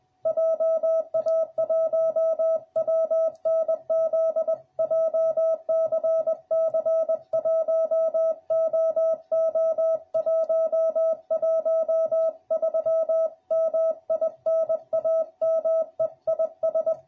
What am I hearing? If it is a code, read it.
Morse code, heard as a keyed tone, about 20 wpm, 650 Hz. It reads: JA1WNZ JCC 1OO113 MI NAME IS